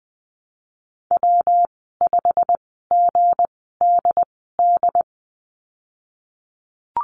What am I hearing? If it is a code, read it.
Morse code, heard as a keyed tone, about 20 wpm, spaced as for 14 wpm, 700 Hz. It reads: W5GDD
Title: 161 Lesson 26g - Letter g - Call Signs 20wpm